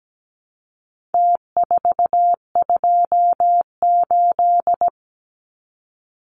Morse code, heard as a keyed tone, about 17 wpm, 700 Hz. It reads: T428
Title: Numbers - Contest Style - 4-Digit 17wpm